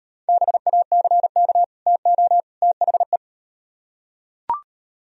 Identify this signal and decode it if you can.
Morse code, heard as a keyed tone, about 38 wpm, 700 Hz. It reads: BACK TO THE